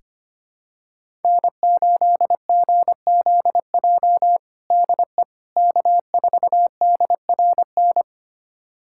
Morse code, heard as a keyed tone, about 25 wpm, 700 Hz. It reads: N8GZJ DE K4DRN
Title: USA Callsign DE Callsign 25wpm